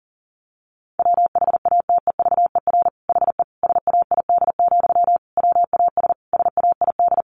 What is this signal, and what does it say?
Morse code, heard as a keyed tone, about 40 wpm, 700 Hz. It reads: WHATEVER HE SAID, WAS SAID
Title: Sentences from Top 1100 Words 40wpm